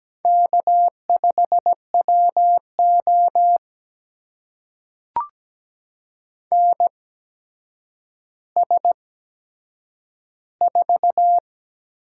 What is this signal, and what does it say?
Morse code, heard as a keyed tone, about 17 wpm, spaced as for 8 wpm, 700 Hz. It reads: K5WO NS4
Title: USA Call Signs - 4 Characters - 8x Character Spacing - 17wpm